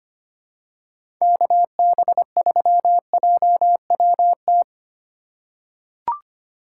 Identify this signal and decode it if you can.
Morse code, heard as a keyed tone, about 25 wpm, 700 Hz. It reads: KB3JWT
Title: USA Call Signs - 6 Characters - 3x Character Spacing - 25wpm